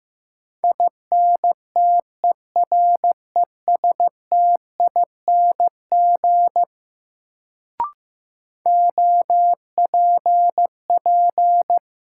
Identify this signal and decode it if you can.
Morse code, heard as a keyed tone, about 15 wpm, 700 Hz. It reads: INTERESTING OPP
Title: Sentences from Top 1000 Words - New Word Drill - 15wpm